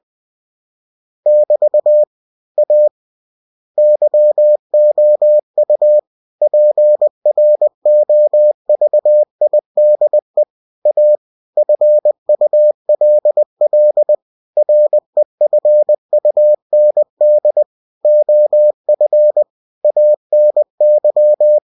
Morse code, heard as a keyed tone, about 20 wpm, 600 Hz. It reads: <BT> A YOU PROVIDE A FULL REFUND OF ANY
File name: cw-books / winnie-the-pooh / winnie-20wpm-page-0117